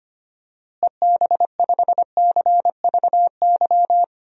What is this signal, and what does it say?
Morse code, heard as a keyed tone, about 25 wpm, 700 Hz. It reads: EB5CVY
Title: DXCC-100 Call Signs - 6 Characters - Repeated 2x - 25wpm